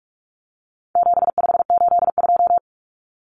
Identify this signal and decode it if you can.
Morse code, heard as a keyed tone, about 44 wpm, 700 Hz. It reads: 7582
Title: Numbers - 4-Digit 45wpm